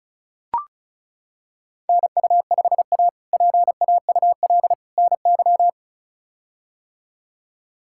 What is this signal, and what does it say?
Morse code, heard as a keyed tone, about 36 wpm, 700 Hz. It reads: NU5A PAUL NY
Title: K1USN SST QSO Practice - 35wpm